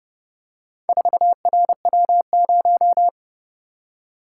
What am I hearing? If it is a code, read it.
Morse code, heard as a keyed tone, about 29 wpm, 700 Hz. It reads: 4RW0